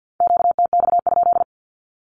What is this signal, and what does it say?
Morse code, heard as a keyed tone, about 54 wpm, 700 Hz. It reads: QTX?
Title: Q-Codes 50wpm